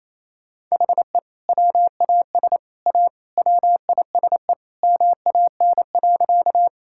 Morse code, heard as a keyed tone, about 28 wpm, 700 Hz. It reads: HE WAS A WISE MAN.
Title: Sentences from Top 1600 Words 28wpm